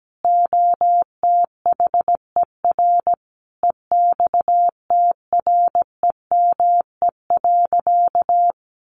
Morse code, heard as a keyed tone, about 17 wpm, 700 Hz. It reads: OTHER EXTREME.
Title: Sentences from Top 1300 Words 17wpm